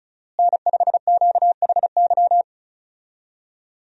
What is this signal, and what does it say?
Morse code, heard as a keyed tone, about 36 wpm, 700 Hz. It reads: N5QHY
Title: USA Call Signs - 5 Characters - Repeated 3x - 35wpm